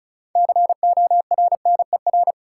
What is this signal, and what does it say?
Morse code, heard as a keyed tone, about 36 wpm, 700 Hz. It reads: CORNER
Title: Sentences from Top 1600 Words - New Word Drill - 35wpm